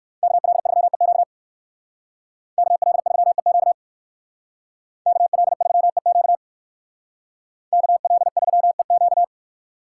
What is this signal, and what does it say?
Morse code, heard as a keyed tone, about 60 wpm, 700 Hz. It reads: KL3EQ KL3EQ KL3EQ KL3EQ